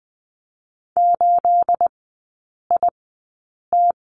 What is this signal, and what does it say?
Morse code, heard as a keyed tone, about 19 wpm, spaced as for 8 wpm, 690 Hz. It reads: 8IT
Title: 212 Lesson 34e - Number 8 - Three Characters 20wpm